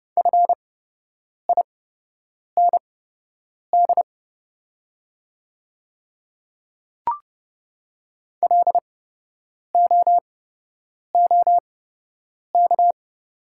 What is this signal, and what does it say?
Morse code, heard as a keyed tone, about 30 wpm, spaced as for 8 wpm, 700 Hz. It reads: FIND LOOK
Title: 8x Character Spacing - Top 100 Words - 30wpm